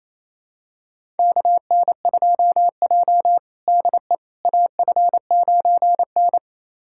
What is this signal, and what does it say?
Morse code, heard as a keyed tone, about 28 wpm, 700 Hz. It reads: KN2J DE AF9N
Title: USA Callsign DE Callsign - 4 Characters - 28wpm